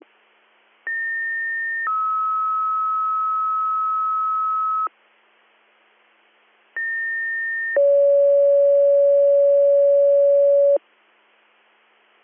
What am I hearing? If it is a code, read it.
Two-tone page, heard as a one-second tone, then a three-second tone, about 1800 then 1290 Hz, about 1800 then 560 Hz. It is heard from 0.9 to 10.8 s.